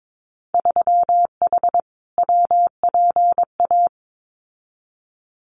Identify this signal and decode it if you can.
Morse code, heard as a keyed tone, about 22 wpm, 700 Hz. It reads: 3H WPA
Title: Winter Field Day Exchanges Encore 22wpm